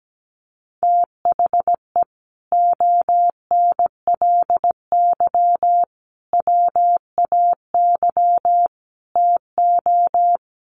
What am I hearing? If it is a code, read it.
Morse code, heard as a keyed tone, about 17 wpm, 700 Hz. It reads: THE ONLY WAY TO
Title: Sets of 4 Words 17wpm